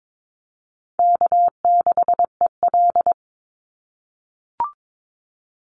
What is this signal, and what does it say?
Morse code, heard as a keyed tone, about 22 wpm, 700 Hz. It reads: K6EL
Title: NA SOTA Chaser Callsigns 22wpm